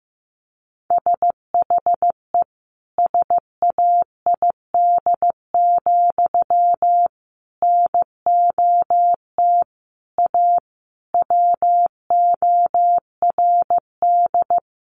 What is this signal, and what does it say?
Morse code, heard as a keyed tone, about 15 wpm, 700 Hz. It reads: SHE SAID, NOT A WORD
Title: Sentences from Top 800 Words 15wpm